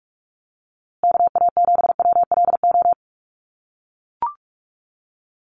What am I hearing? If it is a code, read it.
Morse code, heard as a keyed tone, about 44 wpm, 700 Hz. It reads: KA7WLO
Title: USA Call Signs Repeated 2x 45wpm